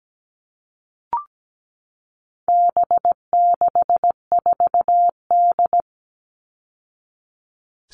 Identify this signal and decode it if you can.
Morse code, heard as a keyed tone, about 17 wpm, 700 Hz. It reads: B64D
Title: All Cut Numbers - 4-Digit 17wpm